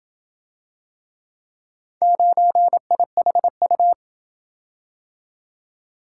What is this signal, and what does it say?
Morse code, heard as a keyed tone, about 28 wpm, 700 Hz. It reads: 9IHU